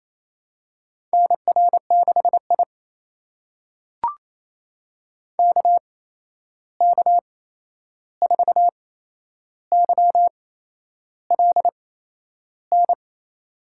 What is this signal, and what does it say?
Morse code, heard as a keyed tone, about 28 wpm, spaced as for 11 wpm, 700 Hz. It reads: NR6I KK4YLN
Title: USA Call Signs - 8x Character Spacing - 28wpm